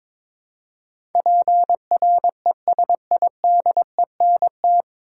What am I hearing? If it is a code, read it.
Morse code, heard as a keyed tone, about 22 wpm, 700 Hz. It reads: PRESIDENT
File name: 9 Letter Words - First 3 Letters Given - 22wpm